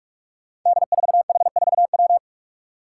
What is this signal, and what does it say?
Morse code, heard as a keyed tone, about 41 wpm, 700 Hz. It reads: D4H4W
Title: Five Challenging Letter-Number Combinations 45wpm